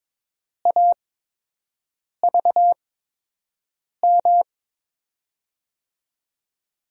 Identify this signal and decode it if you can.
Morse code, heard as a keyed tone, about 22 wpm, spaced as for 6 wpm, 700 Hz. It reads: AVM
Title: USA Call Signs - 8x Character Spacing - 22wpm